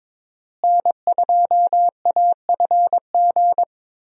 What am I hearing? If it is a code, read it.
Morse code, heard as a keyed tone, about 21 wpm, 700 Hz. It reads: N2AFG
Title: USA Call Signs - 5 Characters - Repeated 3x - 22wpm